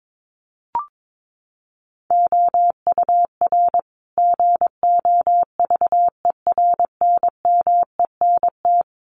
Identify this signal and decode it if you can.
Morse code, heard as a keyed tone, about 22 wpm, 700 Hz. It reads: OUR GOVERNMENT
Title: Sentences from Top 1200 Words 22wpm